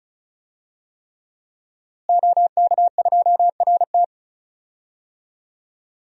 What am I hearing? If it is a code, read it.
Morse code, heard as a keyed tone, about 36 wpm, 700 Hz. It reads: OK2RT
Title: DXCC-100 Call Signs - 5 Characters - Repeated 3x - 35wpm